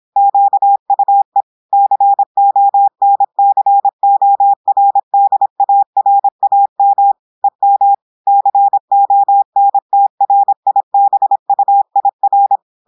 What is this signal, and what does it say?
Morse code, heard as a keyed tone, about 26 wpm, 800 Hz. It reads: QUE CONCORDARAM EM CONTRIBUIR
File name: audios / portugais / portugais texte en clair 1 vitesse 26 mots minutes